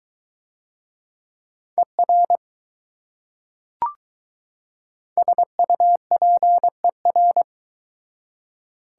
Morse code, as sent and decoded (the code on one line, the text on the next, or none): . .-. / ... ..- .--. .-.
ER SUPER